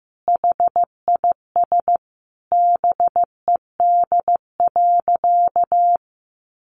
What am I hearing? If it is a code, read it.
Morse code, heard as a keyed tone, about 15 wpm, 700 Hz. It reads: HIS BED.
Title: Sentences from Top 1300 Words 15wpm